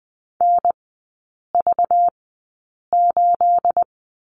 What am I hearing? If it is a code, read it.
Morse code, heard as a keyed tone, about 20 wpm, spaced as for 8 wpm, 690 Hz. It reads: NV8
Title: 212 Lesson 34e - Number 8 - Three Characters 20wpm